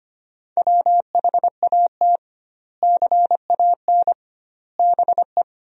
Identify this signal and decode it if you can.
Morse code, heard as a keyed tone, about 25 wpm, 700 Hz. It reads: WHAT CAN BE
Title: 2x Word-Spacing - Sentences from Top 500 Words - 25wpm